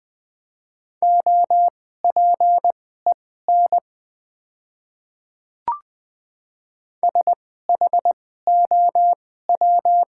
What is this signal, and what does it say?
Morse code, heard as a keyed tone, about 21 wpm, spaced as for 15 wpm, 700 Hz. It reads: OPEN SHOW
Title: 2x Character Spacing - Top 300 Words - 20wpm